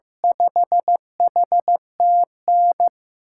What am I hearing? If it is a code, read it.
Morse code, heard as a keyed tone, about 16 wpm, 700 Hz. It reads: 5HTN
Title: USA Call Signs - 6 Characters - 6x Character Spacing - 15wpm